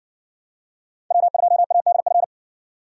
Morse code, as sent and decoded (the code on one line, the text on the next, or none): .-- ..--- .- -.. ..-.
W2ADF